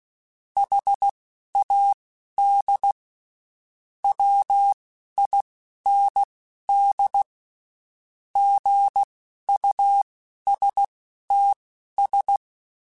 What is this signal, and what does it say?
Morse code, heard as a keyed tone, about 16 wpm, spaced as for 11 wpm, 800 Hz. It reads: HAD WIND GUSTS
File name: Spaced Characters / Spaced Characters-mp3s / Weather 5 5 sp @15